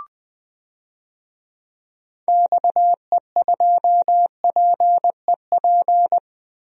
Morse code, heard as a keyed tone, about 20 wpm, 700 Hz. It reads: XE2PEP
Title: DXCC-100 Call Signs - 6 Characters - Repeated 2x - 20wpm